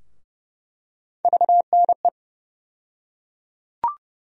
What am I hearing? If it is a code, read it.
Morse code, heard as a keyed tone, about 30 wpm, 700 Hz. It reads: VNE
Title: USA Call Sign Postfixes - 3 Letters 30wpm